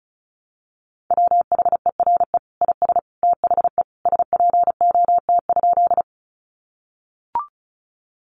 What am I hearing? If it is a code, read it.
Morse code, heard as a keyed tone, about 34 wpm, 700 Hz. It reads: WHERE IS THE SPOT?
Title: Sentences from Top 1100 Words 35wpm